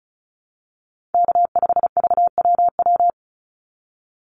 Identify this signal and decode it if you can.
Morse code, heard as a keyed tone, about 34 wpm, 700 Hz. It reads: K5VWW
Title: Call Signs - 35wpm